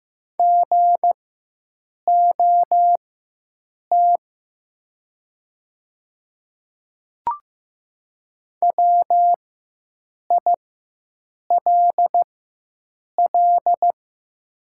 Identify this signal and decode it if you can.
Morse code, heard as a keyed tone, about 15 wpm, spaced as for 7 wpm, 700 Hz. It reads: GOT WILL
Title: 4x Character Spacing - Top 300 Words - 15wpm